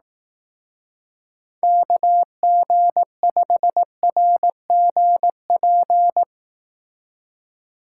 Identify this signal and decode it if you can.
Morse code, heard as a keyed tone, about 18 wpm, 700 Hz. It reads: KG5RGP
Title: Speed Racing - USA 2x3 Call Signs 15wpm